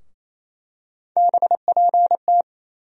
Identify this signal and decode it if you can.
Morse code, heard as a keyed tone, about 28 wpm, 700 Hz. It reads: BPT
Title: USA Call Sign Postfixes - 3 Letters 28wpm